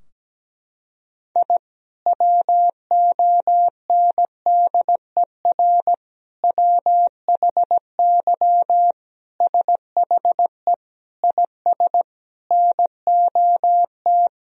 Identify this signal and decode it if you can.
Morse code, heard as a keyed tone, about 17 wpm, 700 Hz. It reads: I WONDER WHY SHE IS NOT
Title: Sentences from Top 1000 Words - Review Part 3 - 17wpm